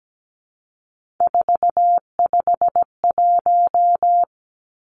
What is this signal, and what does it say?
Morse code, heard as a keyed tone, about 17 wpm, 700 Hz. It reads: 451
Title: Three Challenging Letter-Number Combinations 17wpm